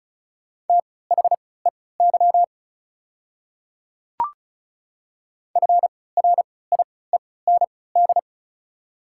Morse code, heard as a keyed tone, about 36 wpm, spaced as for 20 wpm, 700 Hz. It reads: THEY FRIEND
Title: 3x Character Spacing - Top 300 Words - 35wpm